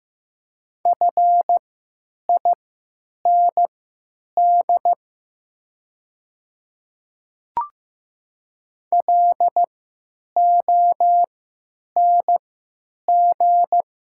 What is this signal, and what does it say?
Morse code, heard as a keyed tone, about 15 wpm, spaced as for 9 wpm, 690 Hz. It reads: FIND LONG
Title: 3x Character Spacing - Top 100 Words - 15wpm